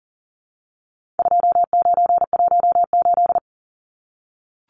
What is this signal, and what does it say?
Morse code, heard as a keyed tone, about 40 wpm, 700 Hz. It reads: 2918
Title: Numbers - 4-Digit 40wpm